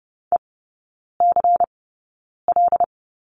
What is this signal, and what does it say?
Morse code, heard as a keyed tone, about 30 wpm, spaced as for 9 wpm, 700 Hz. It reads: ECL
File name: Lesson 16e - Letter c - Three Characters 30wpm